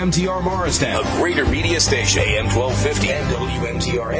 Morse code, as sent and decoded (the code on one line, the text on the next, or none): none